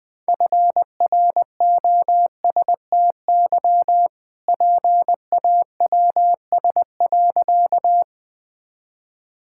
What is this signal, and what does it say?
Morse code, heard as a keyed tone, about 20 wpm, 700 Hz. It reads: FROSTY PAWS.
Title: All-Age Appropriate Jokes - 20wpm